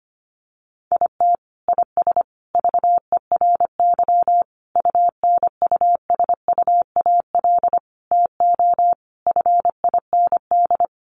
Morse code, heard as a keyed tone, about 25 wpm, 700 Hz. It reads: IT IS VERY UNUSUAL TO FIND
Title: Sentences from Top 1400 Words 25wpm